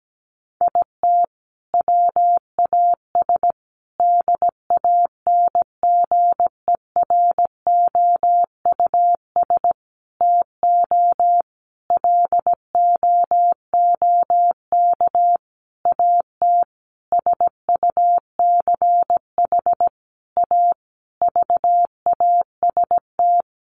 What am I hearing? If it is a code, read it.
Morse code, heard as a keyed tone, about 17 wpm, 700 Hz. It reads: IT WAS DANGEROUS TO LOOK AT SUCH A VAST
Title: Sentences from Top 1300 Words 17wpm